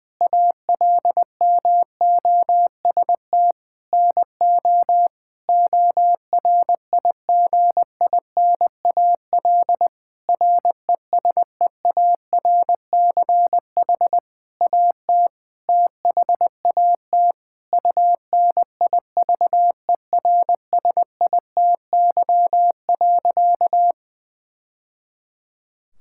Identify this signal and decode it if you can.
Morse code, heard as a keyed tone, about 20 wpm, 700 Hz. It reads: ALMOST NO ORIGINAL RESEARCH AT THAT UNIVERSITY.
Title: Sentences from Top 1000 Words - Review Part 6 - 20wpm